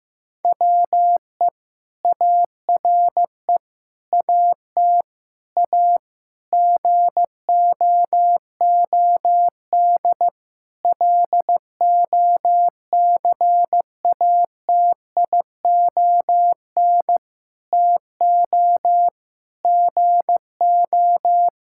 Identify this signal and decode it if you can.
Morse code, heard as a keyed tone, about 15 wpm, 700 Hz. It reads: WE ARE AT A GOOD LOCATION TO GO
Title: Sentences from Top 1000 Words - Review Part 6 - 15wpm